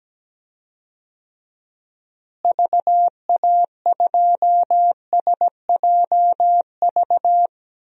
Morse code, heard as a keyed tone, about 17 wpm, 700 Hz. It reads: VA2SJV
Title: DXCC-100 Call Signs - 6 Characters - Repeated 2x - 17wpm